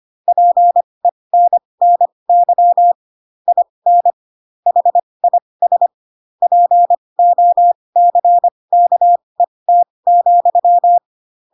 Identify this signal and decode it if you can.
Morse code, heard as a keyed tone, about 25 wpm, spaced as for 20 wpm, 700 Hz. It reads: PENNY IN HIS POCKET,
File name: Pinocchio / Pinocchio ch8 15 25 ik1hge